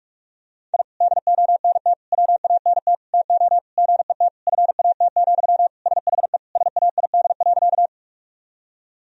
Morse code, heard as a keyed tone, about 41 wpm, 700 Hz. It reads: I DONT WANT TO GET FAT, SHE SAID.
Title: Sentences from Top 1000 Words - Review Part 4 - 45wpm